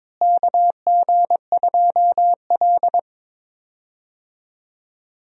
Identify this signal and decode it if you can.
Morse code, heard as a keyed tone, about 22 wpm, 700 Hz. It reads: KG2L
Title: USA Call Signs - 4 Characters - Repeated 2x - 22wpm